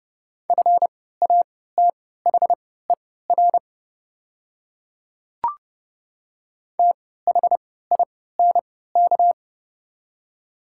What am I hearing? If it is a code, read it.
Morse code, heard as a keyed tone, about 30 wpm, spaced as for 17 wpm, 700 Hz. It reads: FATHER THINK
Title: 3x Character Spacing - Top 300 Words - 30wpm